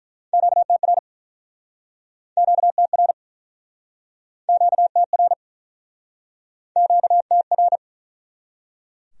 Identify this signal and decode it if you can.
Morse code, heard as a keyed tone, about 47 wpm, 700 Hz. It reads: QTR QTR QTR QTR